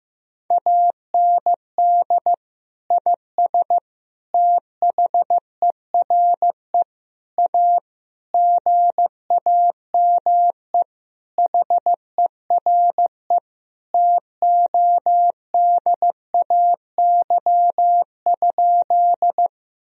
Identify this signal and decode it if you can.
Morse code, heard as a keyed tone, about 15 wpm, 700 Hz. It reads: AND IS THERE A GAME HERE TODAY?
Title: Sentences from Top 1000 Words - Review Part 3 - 15wpm